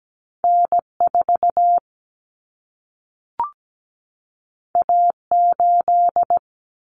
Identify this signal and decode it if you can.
Morse code, heard as a keyed tone, about 17 wpm, 700 Hz. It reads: N4 A8
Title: Common Cut Numbers - 2-Digit 17wpm